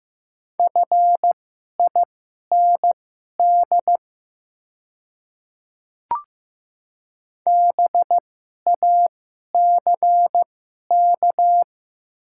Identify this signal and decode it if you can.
Morse code, heard as a keyed tone, about 15 wpm, spaced as for 11 wpm, 690 Hz. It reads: FIND BACK
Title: 2x Character Spacing - Top 500 Words - 15wpm